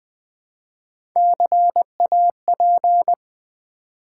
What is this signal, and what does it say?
Morse code, heard as a keyed tone, about 20 wpm, 700 Hz. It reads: CAP
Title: Build A Word - 1 Letter At A Time - 3-Letter Words - 20wpm